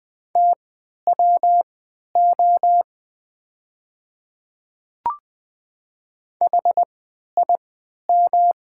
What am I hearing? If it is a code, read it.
Morse code, heard as a keyed tone, about 20 wpm, spaced as for 11 wpm, 690 Hz. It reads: TWO HIM
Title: 3x Character Spacing - Top 100 Words - 20wpm